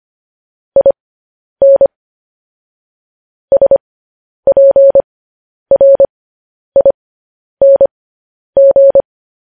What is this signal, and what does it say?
Morse code, heard as a keyed tone, about 25 wpm, spaced as for 10 wpm, 550 Hz. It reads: IN SPRING
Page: news-headlines-in-morse-code-at-10-wpm